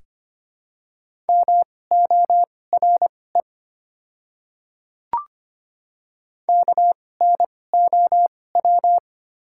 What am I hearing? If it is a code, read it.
Morse code, heard as a keyed tone, about 25 wpm, spaced as for 18 wpm, 690 Hz. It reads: MORE KNOW
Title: 2x Character Spacing - Top 200 Words - 25wpm